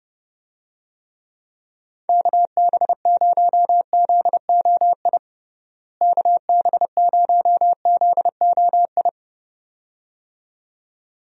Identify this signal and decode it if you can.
Morse code, heard as a keyed tone, about 29 wpm, 700 Hz. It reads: KB0ZOI KB0ZOI
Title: USA Call Signs - 6 Characters - Repeated 2x - 30wpm